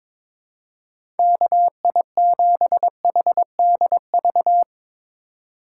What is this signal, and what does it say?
Morse code, heard as a keyed tone, about 22 wpm, 700 Hz. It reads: KI7HDV
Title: USA Call Signs - 6 Characters - 4x Character Spacing - 22wpm